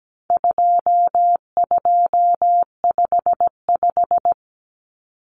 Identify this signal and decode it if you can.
Morse code, heard as a keyed tone, about 17 wpm, 700 Hz. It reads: 2255
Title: Numbers - 4-Digit 17wpm